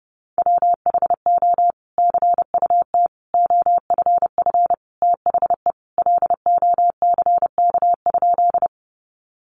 Morse code, heard as a keyed tone, about 30 wpm, 700 Hz. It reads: WHO CUT OFF THE LOCK?
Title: Sentences from Top 1300 Words 30wpm